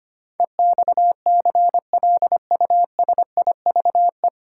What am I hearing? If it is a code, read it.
Morse code, heard as a keyed tone, about 25 wpm, 700 Hz. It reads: EXCLUSIVE
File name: Nine Letter Words - 25wpm